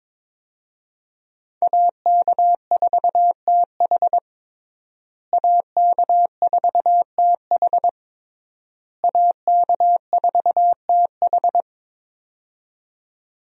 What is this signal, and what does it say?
Morse code, heard as a keyed tone, about 22 wpm, 700 Hz. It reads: AK4TH AK4TH AK4TH
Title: USA Call Signs - 5 Characters - Repeated 3x - 22wpm